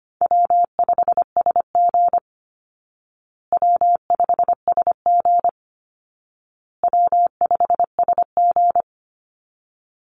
Morse code, heard as a keyed tone, about 25 wpm, 700 Hz. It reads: W5SG W5SG W5SG
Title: Call Signs Repeated 2x - 25wpm